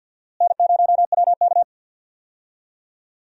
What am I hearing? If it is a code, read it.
Morse code, heard as a keyed tone, about 47 wpm, 700 Hz. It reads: N0WK
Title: USA Call Signs - 4 Characters - Repeated 2x - 50wpm